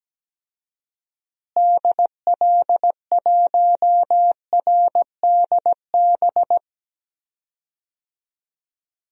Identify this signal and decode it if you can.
Morse code, heard as a keyed tone, about 17 wpm, 700 Hz. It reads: DL1RDB
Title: DXCC-100 Call Signs - 6 Characters - Repeated 3x - 17wpm